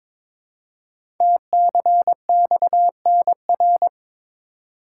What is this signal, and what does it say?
Morse code, heard as a keyed tone, about 22 wpm, 700 Hz. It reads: TCXNR
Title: Code Groups - 5 Letters - 22wpm